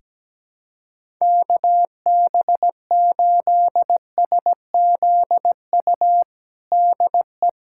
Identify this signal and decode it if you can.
Morse code, heard as a keyed tone, about 17 wpm, 700 Hz. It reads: KB8SZU DE
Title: USA Callsign DE Callsign - 6 Characters - 17wpm